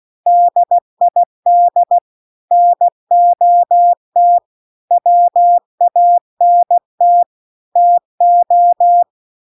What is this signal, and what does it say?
Morse code, heard as a keyed tone, about 16 wpm, 700 Hz. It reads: DID NOT WANT TO